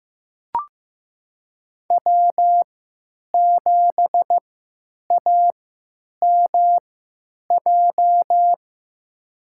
Morse code, heard as a keyed tone, about 15 wpm, spaced as for 9 wpm, 700 Hz. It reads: W7AMJ
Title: USA Call Signs - 5 Characters - 3x Character Spacing - 15wpm